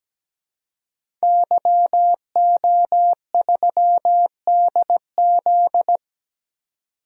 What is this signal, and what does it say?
Morse code, heard as a keyed tone, about 17 wpm, 700 Hz. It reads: YO3DZ
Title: DXCC-100 Call Signs - 5 Characters - 17wpm